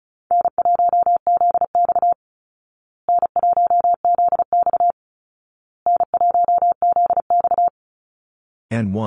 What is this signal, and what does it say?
Morse code, heard as a keyed tone, about 34 wpm, 700 Hz. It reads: N1ZX N1ZX N1ZX
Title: Easy Call Signs Repeated 3x - 35wpm